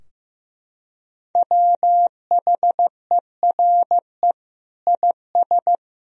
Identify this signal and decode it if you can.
Morse code, heard as a keyed tone, about 15 wpm, 700 Hz. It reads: WHERE IS
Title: Sentences from Top 1000 Words - Review Part 4 - 15wpm